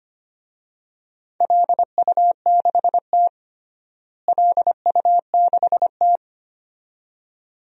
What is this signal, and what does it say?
Morse code, heard as a keyed tone, about 25 wpm, 700 Hz. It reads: LU6T LU6T
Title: DXCC-100 Call Signs - 4 Characters - Repeated 2x - 25wpm